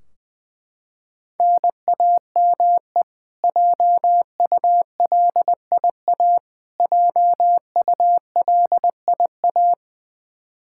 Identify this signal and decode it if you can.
Morse code, heard as a keyed tone, about 19 wpm, 700 Hz. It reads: NAME JULIA JULIA
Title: Practice QSO - Exchange 1 - Beginning - 20wpm